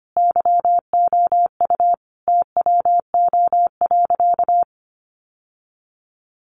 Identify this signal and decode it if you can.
Morse code, heard as a keyed tone, about 25 wpm, 700 Hz. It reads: YOU TWO.
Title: Sentences from Top 700 Words 25wpm